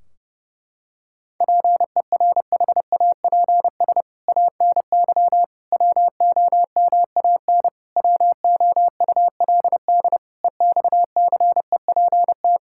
Morse code, heard as a keyed tone, about 30 wpm, 700 Hz. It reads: PERHAPS ANY WOMAN WOULD EXCEPT
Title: Sentences from Top 1600 Words 30wpm